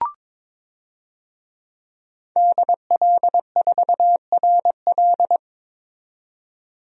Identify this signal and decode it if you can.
Morse code, heard as a keyed tone, about 21 wpm, 700 Hz. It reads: DL4RL
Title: DXCC-100 Call Signs - 5 Characters - 22wpm